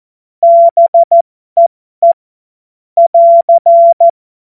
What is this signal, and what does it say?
Morse code, heard as a keyed tone, about 14 wpm, spaced as for 12 wpm, 680 Hz. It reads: BEE <AR>
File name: Morsekurs / Witze